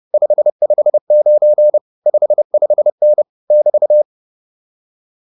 Morse code, heard as a keyed tone, about 30 wpm, 600 Hz. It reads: 559 55N <BT>
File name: QSO109 30